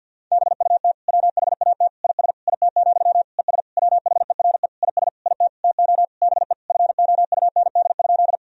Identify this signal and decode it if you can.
Morse code, heard as a keyed tone, about 53 wpm, 700 Hz. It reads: BUT WHAT IS IT, ES WHERE IS IT TO BE FOUND?
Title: Sentences from Top 300 Words and Abbreviations 50wpm